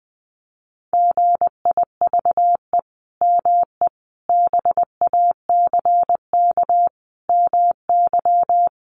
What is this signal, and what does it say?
Morse code, heard as a keyed tone, about 20 wpm, 700 Hz. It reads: GIVE ME BACK MY
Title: Sentences from Top 800 Words 20wpm